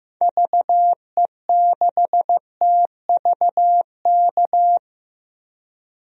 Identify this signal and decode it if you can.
Morse code, heard as a keyed tone, about 15 wpm, 700 Hz. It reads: VE6TVK
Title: DXCC-100 Call Signs - 6 Characters - 15wpm